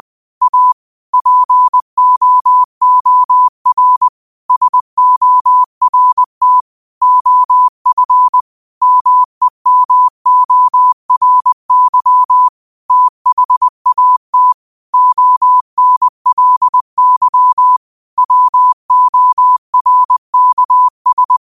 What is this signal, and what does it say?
Morse code, heard as a keyed tone, about 20 wpm, 1000 Hz. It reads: A POOR SORT OF MEMORY THAT ONLY WORKS
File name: qotdfiles / 20wpm